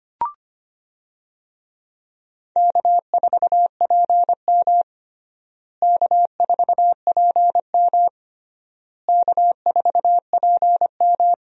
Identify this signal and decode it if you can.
Morse code, heard as a keyed tone, about 25 wpm, 700 Hz. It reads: K4PM K4PM K4PM
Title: USA Call Signs - 4 Characters - Repeated 3x - 25wpm